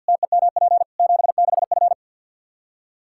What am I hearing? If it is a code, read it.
Morse code, heard as a keyed tone, about 51 wpm, 700 Hz. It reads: TEMP 76F
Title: Practice QSO - Exchange 2 - Info - 50wpm